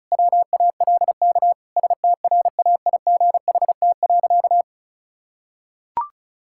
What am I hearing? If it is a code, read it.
Morse code, heard as a keyed tone, about 34 wpm, 700 Hz. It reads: WALK STRAIGHT.
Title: Sentences from Top 1000 Words - Review Part 6 - 35wpm